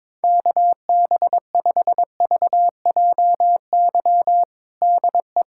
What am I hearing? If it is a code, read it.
Morse code, heard as a keyed tone, about 21 wpm, 700 Hz. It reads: KB5VJY DE
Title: USA Callsign DE Callsign 22wpm